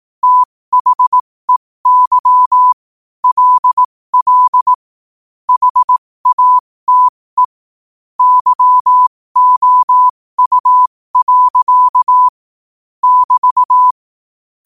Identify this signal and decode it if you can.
Morse code, heard as a keyed tone, about 18 wpm, spaced as for 15 wpm, 1000 Hz. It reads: THEY LL HATE YOU. <BT>
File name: qotdfiles / 15wpm